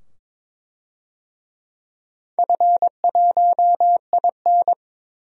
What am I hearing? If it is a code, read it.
Morse code, heard as a keyed tone, about 21 wpm, 700 Hz. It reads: F1IN
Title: DXCC-100 Call Signs - 4 Characters - 22wpm